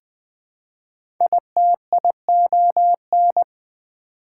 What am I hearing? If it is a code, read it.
Morse code, heard as a keyed tone, about 20 wpm, 700 Hz. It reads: ITION
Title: Prefixes and Suffixes 20wpm